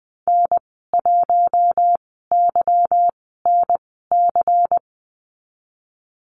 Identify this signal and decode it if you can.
Morse code, heard as a keyed tone, about 20 wpm, spaced as for 15 wpm, 700 Hz. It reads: N1YNC
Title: 147 Lesson 24g - Letter y - Call Signs 20wpm